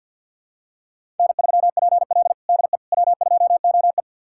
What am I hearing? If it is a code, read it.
Morse code, heard as a keyed tone, about 50 wpm, 700 Hz. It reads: N3PL DE W2OE
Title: USA Callsign DE Callsign - 4 Characters - 50wpm